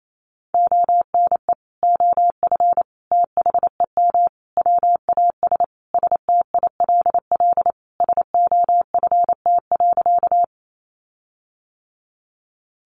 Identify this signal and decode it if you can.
Morse code, heard as a keyed tone, about 28 wpm, 700 Hz. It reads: ONE OF THEM WAS STILL SOFT.
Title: Sentences from Top 1500 Words 28wpm